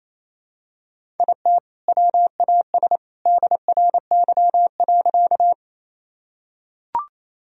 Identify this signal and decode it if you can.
Morse code, heard as a keyed tone, about 28 wpm, 700 Hz. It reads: IT WAS DRY.
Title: Sentences from Top 1000 Words - Review Part 3 - 28wpm